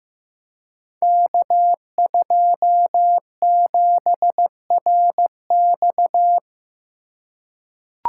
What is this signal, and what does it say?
Morse code, heard as a keyed tone, about 15 wpm, 700 Hz. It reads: K27RX
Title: Code Groups - 5 Letter:Numbers - 15wpm